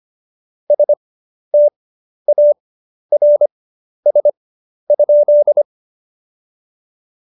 Morse code, heard as a keyed tone, about 25 wpm, spaced as for 11 wpm, 590 Hz. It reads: STARS?
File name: Jokes-25-char-10-eff